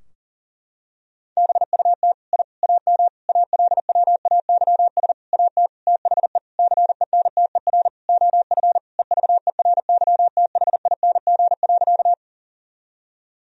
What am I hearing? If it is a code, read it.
Morse code, heard as a keyed tone, about 40 wpm, 700 Hz. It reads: BUT I AM ALWAYS AT THE CENTER OF EVERYTHING.
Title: Sentences from Top 1000 Words - Review Part 2 - 40wpm